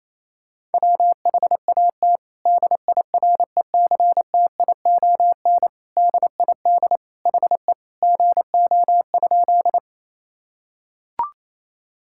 Morse code, heard as a keyed tone, about 28 wpm, 700 Hz. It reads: WHAT DIRECTION DID HE GO?
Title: Sentences from Top 1000 Words - Review Part 5 - 28wpm